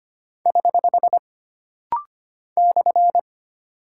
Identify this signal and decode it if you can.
Morse code, heard as a keyed tone, about 25 wpm, 700 Hz. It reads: <HH> /
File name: Single Punctuation-Prosign - Rapid-Fire - 25wpm